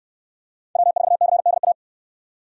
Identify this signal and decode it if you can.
Morse code, heard as a keyed tone, about 68 wpm, 700 Hz. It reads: W4CDA